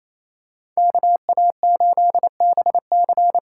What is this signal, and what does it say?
Morse code, heard as a keyed tone, about 28 wpm, 700 Hz. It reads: KA8BC
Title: USA Call Signs - 6 Characters - 8x Character Spacing - 28wpm